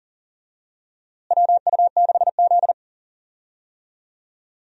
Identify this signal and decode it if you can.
Morse code, heard as a keyed tone, about 40 wpm, 700 Hz. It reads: WU6Z
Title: USA Call Signs - 4 Characters - Repeated 2x - 40wpm